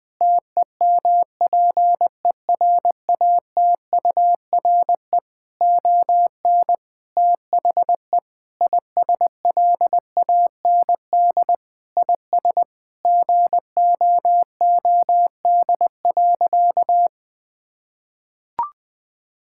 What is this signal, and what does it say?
Morse code, heard as a keyed tone, about 20 wpm, 700 Hz. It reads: TEMPERATURE ON THE ISLAND IS GOOD.
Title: Sentences from Top 1000 Words - Review Part 6 - 20wpm